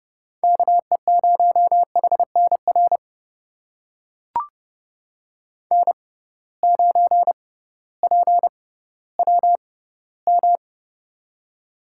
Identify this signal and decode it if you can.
Morse code, heard as a keyed tone, about 29 wpm, spaced as for 16 wpm, 700 Hz. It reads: KE0HNR N9PWM
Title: USA Call Signs - 6x Character Spacing - 30wpm